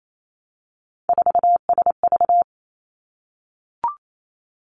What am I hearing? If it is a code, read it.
Morse code, heard as a keyed tone, about 28 wpm, 700 Hz. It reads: 4SV
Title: Three Challenging Letter-Number Combinations 28wpm